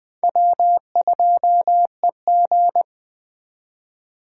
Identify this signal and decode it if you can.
Morse code, heard as a keyed tone, about 20 wpm, 700 Hz. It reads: W2EG